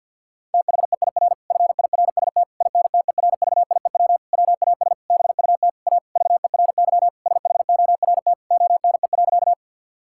Morse code, heard as a keyed tone, about 50 wpm, 700 Hz. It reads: THEIR FIRST INTERVIEW WAS BUT A VERY SHORT ONE.
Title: Sentences from Top 1600 Words 50wpm